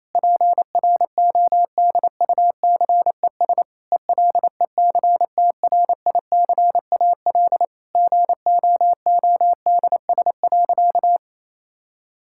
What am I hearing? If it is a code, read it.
Morse code, heard as a keyed tone, about 28 wpm, 700 Hz. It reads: PRODUCES ELECTRICAL GOODS.
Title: Sentences from Top 1000 Words - Review Part 6 - 28wpm